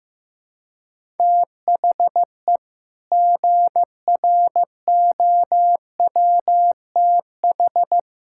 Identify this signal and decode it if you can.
Morse code, heard as a keyed tone, about 15 wpm, 700 Hz. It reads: THE GROWTH
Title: Sentences from Top 1000 Words - Review Part 5 - 15wpm